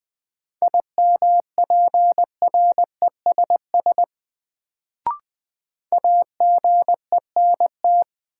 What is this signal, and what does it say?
Morse code, heard as a keyed tone, about 20 wpm, 700 Hz. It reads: IMPRESS AGENT
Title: Sentences from Top 1600 Words - New Word Drill - 20wpm